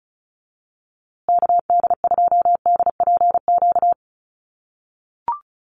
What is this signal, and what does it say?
Morse code, heard as a keyed tone, about 34 wpm, 700 Hz. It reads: KD2DPQ
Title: USA Call Signs Repeated 2x 35wpm